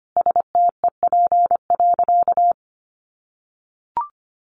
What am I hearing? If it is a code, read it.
Morse code, heard as a keyed tone, about 25 wpm, 700 Hz. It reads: STEP.
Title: Sentences from Top 1400 Words 25wpm